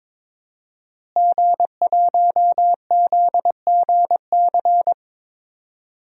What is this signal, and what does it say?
Morse code, heard as a keyed tone, about 22 wpm, 700 Hz. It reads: G1ZGC
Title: DXCC-100 Call Signs - 5 Characters - Repeated 3x - 22wpm